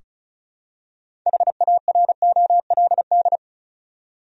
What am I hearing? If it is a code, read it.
Morse code, heard as a keyed tone, about 34 wpm, 700 Hz. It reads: HAROLD
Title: USA Common Names - 35wpm